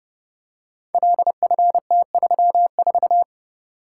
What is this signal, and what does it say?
Morse code, heard as a keyed tone, about 29 wpm, 700 Hz. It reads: LFT34
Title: Code Groups - 5 Letter:Numbers - 30wpm